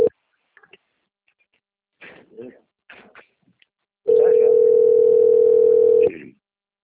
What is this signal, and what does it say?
Ringback tone (North America), 440 + 480 Hz.